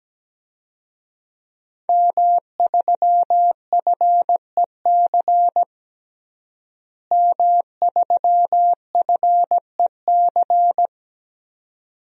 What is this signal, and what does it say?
Morse code, heard as a keyed tone, about 17 wpm, 700 Hz. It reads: M3FEC M3FEC
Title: DXCC-100 Call Signs - 5 Characters - Repeated 2x - 17wpm